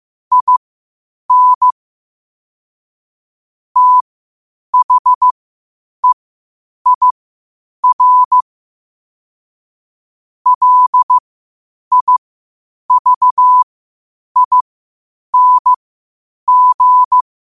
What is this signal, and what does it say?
Morse code, heard as a keyed tone, about 15 wpm, spaced as for 8 wpm, 1000 Hz. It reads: IN THEIR LIVING
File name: qotdfiles / 8wpm